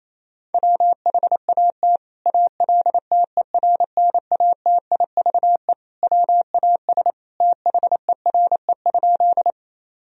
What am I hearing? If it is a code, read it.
Morse code, heard as a keyed tone, about 28 wpm, 700 Hz. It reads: WHAT ALTERNATIVE WAS THERE?
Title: Sentences from Top 1000 Words - Review Part 4 - 28wpm